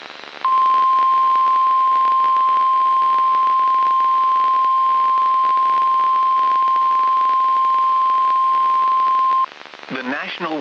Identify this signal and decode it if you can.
Emergency Alert System (SAME) tones (the 1050 Hz attention tone), with speech. It is heard from 0.4 to 9.5 s.